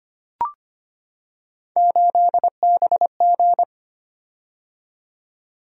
Morse code, heard as a keyed tone, about 25 wpm, 700 Hz. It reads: 8BG
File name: Three Letter-Number Combinations - 25wpm